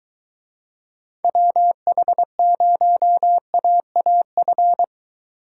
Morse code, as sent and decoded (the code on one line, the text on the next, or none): .-- .... ----- .- .- ..-.
WH0AAF